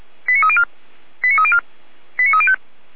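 Five-tone selective call (ZVEI), heard as quick runs of five tones, tones of about 70 ms. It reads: 89304 89304 89305
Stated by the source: ZVEI five-tone call 89246 received by SDR on channel 455.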